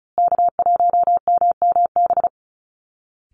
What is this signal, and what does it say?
Morse code, heard as a keyed tone, about 34 wpm, 700 Hz. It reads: K1MMB
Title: Call Signs Repeated 2x - 35wpm